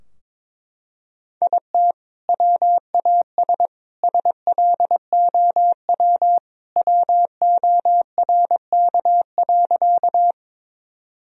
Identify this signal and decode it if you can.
Morse code, heard as a keyed tone, about 21 wpm, 700 Hz. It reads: IT WAS SLOW WORK.
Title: Sentences from Top 1000 Words - Review Part 2 - 22wpm